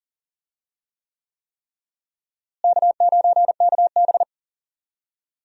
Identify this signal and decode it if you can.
Morse code, heard as a keyed tone, about 40 wpm, 700 Hz. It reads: K9KB
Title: USA Call Signs - 4 Characters - Repeated 3x - 40wpm